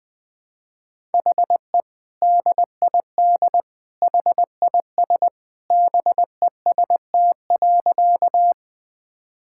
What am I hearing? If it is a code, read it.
Morse code, heard as a keyed tone, about 20 wpm, 700 Hz. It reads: HE DID HIS BEST.
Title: Sentences with 2-Word Phrases - 20wpm